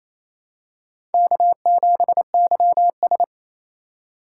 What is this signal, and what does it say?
Morse code, heard as a keyed tone, about 28 wpm, 700 Hz. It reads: K7YS
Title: USA Call Signs - 4 Characters - 4x Character Spacing - 28wpm